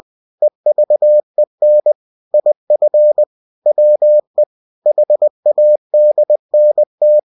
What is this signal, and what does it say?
Morse code, heard as a keyed tone, about 20 wpm, 600 Hz. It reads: EVEN IF WE HADNT
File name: cw-books / winnie-the-pooh / winnie-20wpm-page-0061